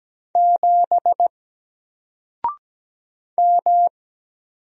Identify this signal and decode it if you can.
Morse code, heard as a keyed tone, about 17 wpm, 700 Hz. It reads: 7 M
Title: Single Character (Letter:Number:Punctuation:Prosign) Alt 17wpm